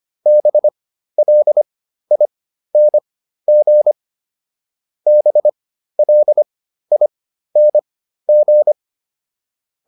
Morse code, heard as a keyed tone, about 25 wpm, spaced as for 13 wpm, 600 Hz. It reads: BLING BLING